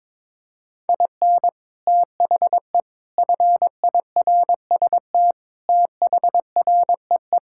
Morse code, heard as a keyed tone, about 22 wpm, 700 Hz. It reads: IN THE FIRST THREE
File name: practice / 22wpm / Sets of 8 Words - 22wpm